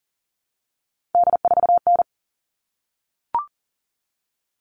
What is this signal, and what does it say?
Morse code, heard as a keyed tone, about 40 wpm, 700 Hz. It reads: D4N
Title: All Cut Numbers - 3-Digit 40wpm